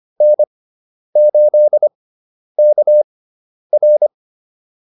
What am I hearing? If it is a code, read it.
Morse code, heard as a keyed tone, about 25 wpm, spaced as for 10 wpm, 600 Hz. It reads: N8KR
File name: QSO206 10